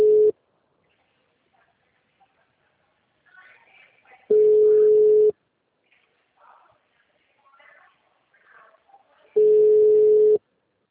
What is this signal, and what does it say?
Ringback tone (Europe, 425 Hz), with speech.